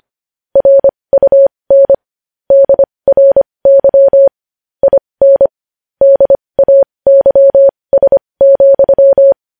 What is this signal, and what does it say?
Morse code, heard as a keyed tone, about 25 wpm, spaced as for 20 wpm, 550 Hz. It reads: RUN DRY IN DAYS,
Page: news-headlines-in-morse-code-at-20-wpm